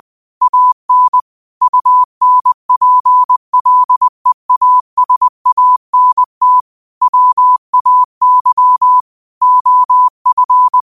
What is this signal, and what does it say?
Morse code, heard as a keyed tone, about 20 wpm, 1000 Hz. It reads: AN UNPLEASANT WAY OF
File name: qotdfiles / 20wpm